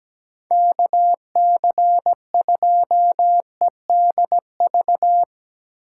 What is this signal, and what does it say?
Morse code, heard as a keyed tone, about 17 wpm, 700 Hz. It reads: KC2EDV
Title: USA Call Signs - 6 Characters - 8x Character Spacing - 17wpm